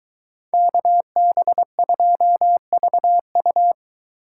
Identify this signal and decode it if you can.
Morse code, heard as a keyed tone, about 23 wpm, 700 Hz. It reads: KB2VU